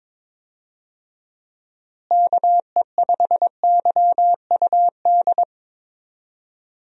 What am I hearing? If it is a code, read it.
Morse code, heard as a keyed tone, about 22 wpm, 700 Hz. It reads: KE5YUD
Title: USA Call Signs - 6 Characters - Repeated 2x - 22wpm